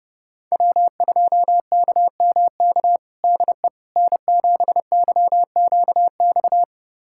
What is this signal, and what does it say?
Morse code, heard as a keyed tone, about 30 wpm, 700 Hz. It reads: W2KMK DE N7YQX
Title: USA Callsign DE Callsign - 5 Characters - 30wpm